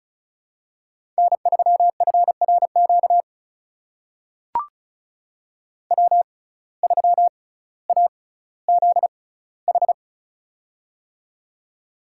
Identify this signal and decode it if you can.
Morse code, heard as a keyed tone, about 36 wpm, spaced as for 18 wpm, 700 Hz. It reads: N3FRQ W3AZH
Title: USA Call Signs - 5 Characters - 6x Character Spacing - 35wpm